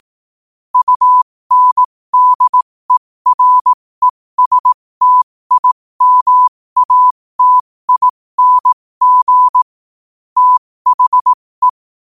Morse code, heard as a keyed tone, about 18 wpm, spaced as for 15 wpm, 1000 Hz. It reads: UNDERESTIMATING THE
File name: qotdfiles / 15wpm